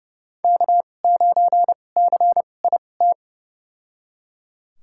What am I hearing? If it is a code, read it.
Morse code, heard as a keyed tone, about 32 wpm, spaced as for 22 wpm, 700 Hz. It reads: K9CIT
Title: USA Call Signs - 5 Characters - 2x Character Spacing - 30wpm